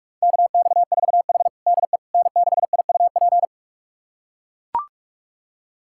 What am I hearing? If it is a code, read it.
Morse code, heard as a keyed tone, about 42 wpm, 700 Hz. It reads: KX4H DE N6IUP
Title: USA Callsign DE Callsign 45wpm